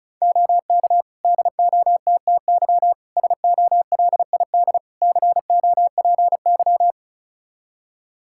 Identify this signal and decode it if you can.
Morse code, heard as a keyed tone, about 36 wpm, 700 Hz. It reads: OK DOTTY SOLID COPY
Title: K1USN SST QSO Practice 35wpm